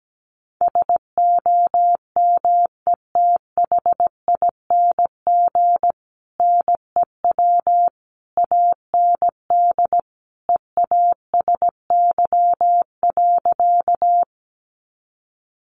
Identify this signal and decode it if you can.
Morse code, heard as a keyed tone, about 17 wpm, 700 Hz. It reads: SOMETHING NEW AND EASY.
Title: Sentences from Top 700 Words 17wpm